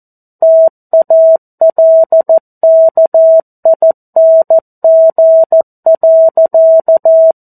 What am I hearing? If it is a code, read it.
Morse code, heard as a keyed tone, about 14 wpm, 650 Hz. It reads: TALKING.
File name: 02-Chapter-02 14 WPM